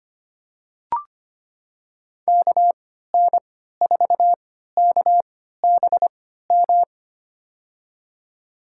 Morse code, heard as a keyed tone, about 25 wpm, spaced as for 14 wpm, 700 Hz. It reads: KN4KBM
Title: USA Call Signs - 6 Characters - 3x Character Spacing - 25wpm